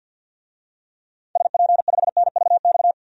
Morse code, heard as a keyed tone, about 52 wpm, 700 Hz. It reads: SP5NVX
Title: DXCC-100 Call Signs - 6 Characters - 50wpm